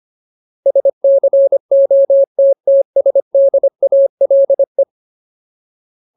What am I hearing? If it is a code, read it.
Morse code, heard as a keyed tone, about 25 wpm, 550 Hz. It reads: SCOTTSDALE
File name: Ultimate Word List / 25WPM Ultimate